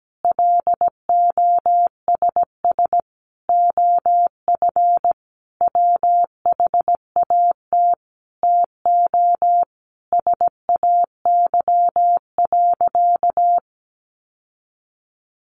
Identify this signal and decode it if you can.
Morse code, heard as a keyed tone, about 17 wpm, 700 Hz. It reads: LOSS OF WHAT TO SAY.
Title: Sentences from Top 1400 Words 17wpm